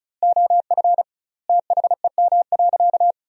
Morse code, heard as a keyed tone, about 36 wpm, 700 Hz. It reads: OF THEM.
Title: 2x Word-Spacing - Sentences from Top 300 Words - 35wpm